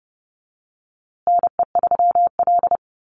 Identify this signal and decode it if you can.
Morse code, heard as a keyed tone, about 30 wpm, 700 Hz. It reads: NE3L
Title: USA Call Signs Repeated 3x 30wpm